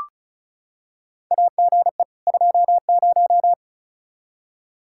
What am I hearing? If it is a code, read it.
Morse code, heard as a keyed tone, about 34 wpm, 700 Hz. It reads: AGE 20
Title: Practice QSO - Exchange 3 - More Info - 35wpm